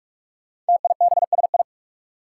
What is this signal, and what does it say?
Morse code, heard as a keyed tone, about 44 wpm, 700 Hz. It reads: TIBSI
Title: Five Challenging Letter-Number Combinations 45wpm